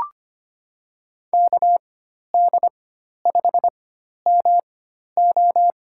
Morse code, heard as a keyed tone, about 25 wpm, 700 Hz. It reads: K D 5 M O
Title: USA Call Signs - 4x Character Spacing - 25wpm